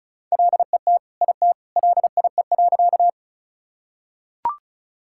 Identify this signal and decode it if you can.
Morse code, heard as a keyed tone, about 36 wpm, 700 Hz. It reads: LET IT LIE.
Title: Sentences from Top 1600 Words 35wpm